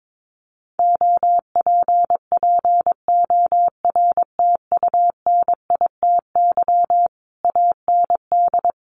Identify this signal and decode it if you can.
Morse code, heard as a keyed tone, about 22 wpm, 700 Hz. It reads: OPPORTUNITY AND
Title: Sentences from Top 1200 Words 22wpm